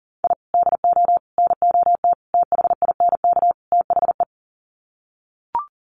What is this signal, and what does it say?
Morse code, heard as a keyed tone, about 40 wpm, 700 Hz. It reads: I DO NOT THINK THE
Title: Sets of 5 Words 40wpm